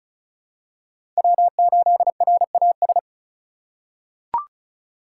Morse code, heard as a keyed tone, about 36 wpm, 700 Hz. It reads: W8RAS
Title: USA Call Signs - 2x Character Spacing - 35wpm